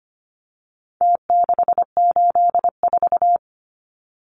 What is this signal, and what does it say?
Morse code, heard as a keyed tone, about 25 wpm, 700 Hz. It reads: T684
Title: Numbers - Contest Style - 4-Digit 25wpm